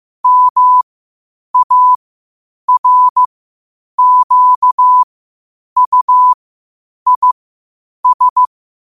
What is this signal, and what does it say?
Morse code, heard as a keyed tone, about 15 wpm, spaced as for 8 wpm, 1000 Hz. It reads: MARQUIS
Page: New quotes every day in morse code at 8 Words per minute.